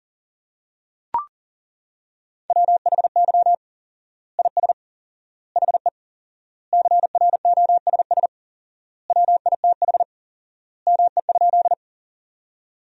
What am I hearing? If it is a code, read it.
Morse code, heard as a keyed tone, about 40 wpm, 700 Hz. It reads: WHY IS HE CROSS WITH ME?
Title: 4x Word-Spacing - Sentences from Top 300 Words - 40wpm